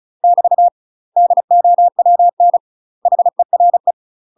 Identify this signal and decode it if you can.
Morse code, heard as a keyed tone, about 34 wpm, 700 Hz. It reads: <BT> DOWN HERE
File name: UK MASTER QSOs TRAINING / 35WPM / UK MASTER QSOs 0000